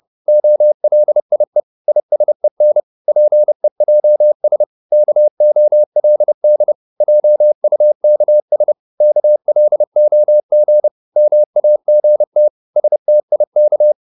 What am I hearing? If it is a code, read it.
Morse code, heard as a keyed tone, about 30 wpm, 600 Hz. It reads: OLIE ISEN PEJS KOLD JUKS KLOG MAGT STIK
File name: Korte ord DK 30wpm